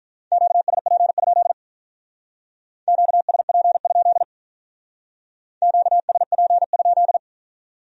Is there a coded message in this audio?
Yes: Morse code from 0.3 to 7.2 s, heard as a keyed tone, about 44 wpm, 700 Hz. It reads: QSP? QSP? QSP?